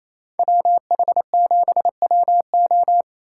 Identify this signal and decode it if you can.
Morse code, heard as a keyed tone, about 36 wpm, 700 Hz. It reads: WH7WO